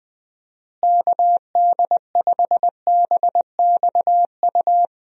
Morse code, heard as a keyed tone, about 21 wpm, 700 Hz. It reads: KD5BXU
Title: USA Call Signs - 6 Characters - Repeated 3x - 20wpm